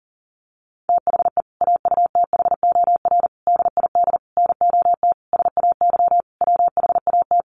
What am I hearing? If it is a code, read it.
Morse code, heard as a keyed tone, about 37 wpm, 700 Hz. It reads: THE AUTHOR DID NOT SAY WHAT
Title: Sentences from Top 1200 Words 40wpm